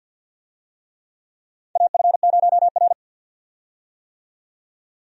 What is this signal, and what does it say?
Morse code, heard as a keyed tone, about 47 wpm, 700 Hz. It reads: AF0R
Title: USA Call Signs - 4 Characters - 50wpm